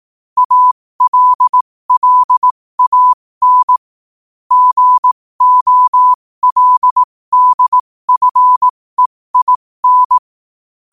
Morse code, heard as a keyed tone, about 18 wpm, spaced as for 15 wpm, 1000 Hz. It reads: ALLAN GOLDFEIN
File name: qotdfiles / 15wpm